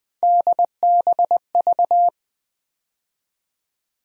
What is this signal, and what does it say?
Morse code, heard as a keyed tone, about 20 wpm, 700 Hz. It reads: DBV
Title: USA Call Sign Postfixes - 3 Letters 20wpm